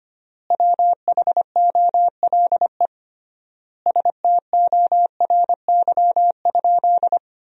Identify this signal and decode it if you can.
Morse code, heard as a keyed tone, about 25 wpm, 700 Hz. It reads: WHOLE STORY?
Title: 3x Word-Spacing - Sentences from Top 400 Words - 25wpm